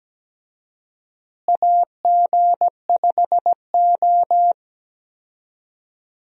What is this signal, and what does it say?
Morse code, heard as a keyed tone, about 17 wpm, 700 Hz. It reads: AG5O
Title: USA Call Signs - 4 Characters - Repeated 2x - 17wpm